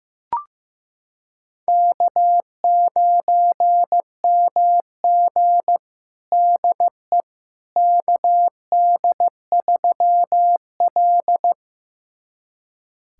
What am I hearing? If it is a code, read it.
Morse code, heard as a keyed tone, about 15 wpm, 700 Hz. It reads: K9MG DE KD3L
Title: USA Callsign DE Callsign - 4 Characters - 15wpm